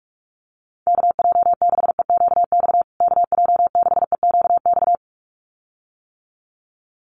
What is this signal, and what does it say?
Morse code, heard as a keyed tone, about 44 wpm, 700 Hz. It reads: KJ6EQX KJ6EQX
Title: USA Call Signs Repeated 2x 45wpm